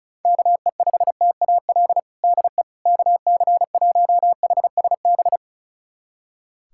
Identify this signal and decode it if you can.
Morse code, heard as a keyed tone, about 36 wpm, 700 Hz. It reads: KE5TAL DE KC1HSB
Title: USA Callsign DE Callsign - 6 Characters - 35wpm